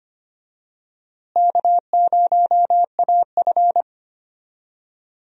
Morse code, heard as a keyed tone, about 25 wpm, 700 Hz. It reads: K0AF
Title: USA Call Signs - 4 Characters - Repeated 2x - 25wpm